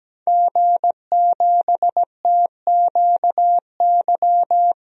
Morse code, heard as a keyed tone, about 17 wpm, 700 Hz. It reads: G7TQY
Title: DXCC-100 Call Signs - 5 Characters - Repeated 3x - 17wpm